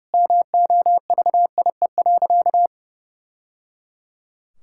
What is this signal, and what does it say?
Morse code, heard as a keyed tone, about 30 wpm, 700 Hz. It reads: MOVIE.
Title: Sentences from Top 1000 Words - Review Part 5 - 30wpm